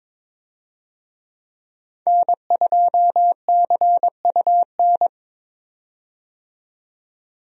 Morse code, heard as a keyed tone, about 21 wpm, 700 Hz. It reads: N2CUN
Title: USA Call Signs - 5 Characters - 22wpm